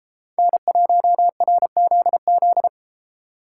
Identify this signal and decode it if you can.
Morse code, heard as a keyed tone, about 32 wpm, 700 Hz. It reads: N1RZZ